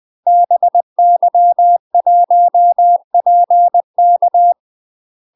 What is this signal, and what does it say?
Morse code, heard as a keyed tone, about 20 wpm, 700 Hz. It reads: BY1PK